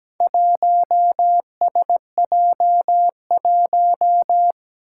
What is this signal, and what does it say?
Morse code, heard as a keyed tone, about 17 wpm, 700 Hz. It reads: 1SJ1
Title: Four Challenging Letter-Number Combinations 17wpm